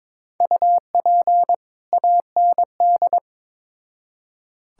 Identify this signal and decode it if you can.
Morse code, heard as a keyed tone, about 21 wpm, 700 Hz. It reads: UP AND
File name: Build A Phrase - 2-Word Phrases - 22wpm